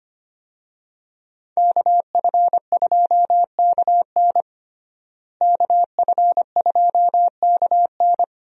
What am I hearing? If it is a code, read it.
Morse code, heard as a keyed tone, about 25 wpm, 700 Hz. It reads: KF2KN KF2KN
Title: USA Call Signs - 5 Characters - Repeated 2x - 25wpm